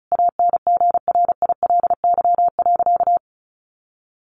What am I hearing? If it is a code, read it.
Morse code, heard as a keyed tone, about 34 wpm, 700 Hz. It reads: ANGRILY.
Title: A Sound Of Thunder 35wpm